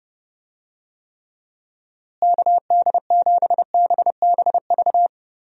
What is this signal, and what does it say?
Morse code, heard as a keyed tone, about 30 wpm, 700 Hz. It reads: KD7BBV
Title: USA Call Signs - 6 Characters - Repeated 2x - 30wpm